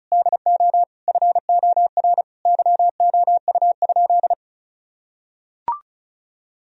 Morse code, heard as a keyed tone, about 34 wpm, 700 Hz. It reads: DO FOR YOU?
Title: Sentences from Top 1000 Words - Review Part 2 - 35wpm